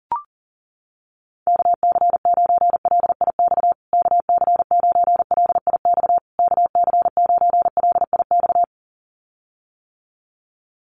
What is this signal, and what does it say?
Morse code, heard as a keyed tone, about 37 wpm, 700 Hz. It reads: KC9LIX KC9LIX KC9LIX
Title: USA Call Signs Repeated 3x 40wpm